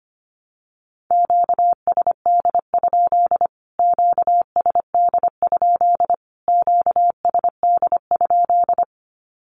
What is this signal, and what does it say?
Morse code, heard as a keyed tone, about 25 wpm, 700 Hz. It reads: QSD? QSD? QSD?
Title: Q-Codes Repeated 3x 25wpm